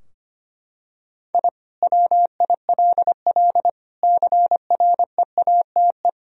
Morse code, heard as a keyed tone, about 25 wpm, 700 Hz. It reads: I WILL CREATE
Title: Sentences from Top 1000 Words - Review Part 3 - 25wpm